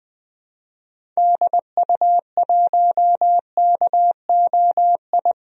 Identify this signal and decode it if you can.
Morse code, heard as a keyed tone, about 20 wpm, 700 Hz. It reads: DU1KOI
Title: DXCC-100 Call Signs - 6 Characters - 20wpm